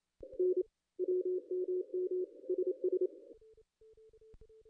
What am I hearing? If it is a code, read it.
Morse code, heard as a keyed tone, about 30 wpm, 370 Hz. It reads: N WMM SS
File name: contests / ss-cw-k5tr-op-n5rz -1900